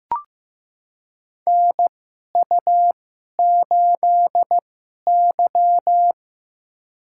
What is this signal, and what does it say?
Morse code, heard as a keyed tone, about 15 wpm, spaced as for 11 wpm, 700 Hz. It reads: NU8Y
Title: USA Call Signs - 4 Characters - 2x Character Spacing - 15wpm